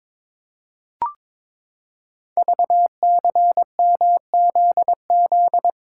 Morse code, heard as a keyed tone, about 22 wpm, 700 Hz. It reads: VCMZZ
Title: Code Groups - 5 Letters - 22wpm